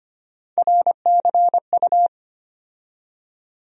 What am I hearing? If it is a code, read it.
Morse code, heard as a keyed tone, about 25 wpm, 700 Hz. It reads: RCU
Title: USA Call Sign Postfixes - 3 Letters 25wpm